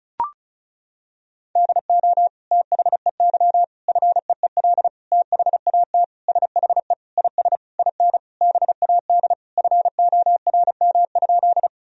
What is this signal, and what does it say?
Morse code, heard as a keyed tone, about 36 wpm, 700 Hz. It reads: DO THEY FEEL THAT SHE IS IN BAD FORM?
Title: Sentences from Top 1000 Words - Review Part 1 - 35wpm